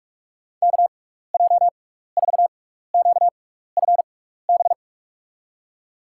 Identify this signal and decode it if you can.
Morse code, heard as a keyed tone, about 45 wpm, 700 Hz. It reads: K J 4 Q F B
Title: USA Call Signs - 6 Characters - 6x Character Spacing - 45wpm